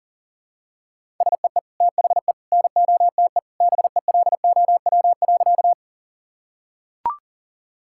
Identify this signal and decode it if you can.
Morse code, heard as a keyed tone, about 40 wpm, 700 Hz. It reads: SEE THE NOTE BELOW.
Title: Sentences from Top 1000 Words - Review Part 5 - 40wpm